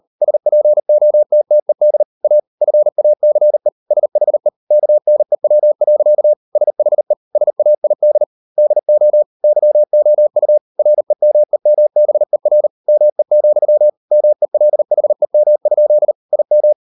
Morse code, heard as a keyed tone, about 38 wpm, 600 Hz. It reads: SPOTTED A FACE SHE KNEW. SHE SAID DO YOU REMEMBER ME, MELHEM? IM